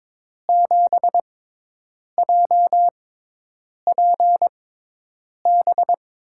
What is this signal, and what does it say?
Morse code, heard as a keyed tone, about 22 wpm, spaced as for 8 wpm, 700 Hz. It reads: 7JPB
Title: USA Call Signs - 6 Characters - 6x Character Spacing - 22wpm